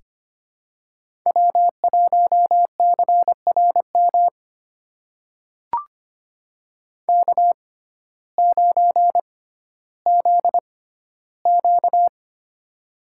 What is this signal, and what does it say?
Morse code, heard as a keyed tone, about 25 wpm, spaced as for 14 wpm, 700 Hz. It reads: W1CRM K9ZQ
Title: USA Call Signs - 6x Character Spacing - 25wpm